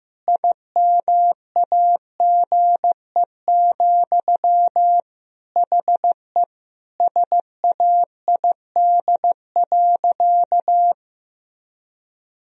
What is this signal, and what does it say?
Morse code, heard as a keyed tone, about 16 wpm, 700 Hz. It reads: IMAGE, HE SAID.
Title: Sentences from Top 1000 Words - Review Part 4 - 15wpm